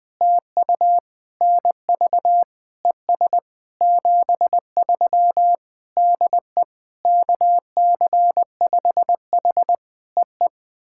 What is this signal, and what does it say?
Morse code, heard as a keyed tone, about 20 wpm, 700 Hz. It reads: TU NV ES 73 DE KC5H EE
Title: POTA QSO Practice - 20wpm